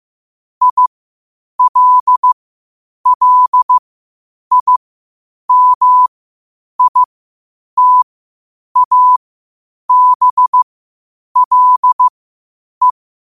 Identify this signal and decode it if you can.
Morse code, heard as a keyed tone, about 15 wpm, spaced as for 8 wpm, 1000 Hz. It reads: ILLIMITABLE
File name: qotdfiles / 8wpm